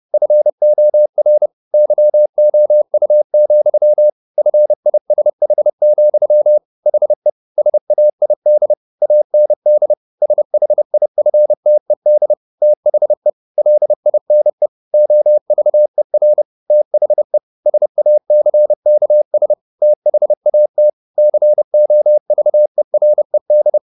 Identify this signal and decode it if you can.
Morse code, heard as a keyed tone, about 30 wpm, 600 Hz. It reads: FOR YOU, FISH, HE SAID AND SHIFTED THE LINE OVER THE SACKS THAT COVERED
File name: the old man and the sea / the old man and the sea 1 30wpm